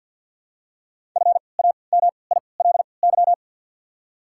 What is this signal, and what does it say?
Morse code, heard as a keyed tone, about 48 wpm, spaced as for 28 wpm, 700 Hz. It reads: FAMILY
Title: 3x Character Spacing - Top 500 Words - 50wpm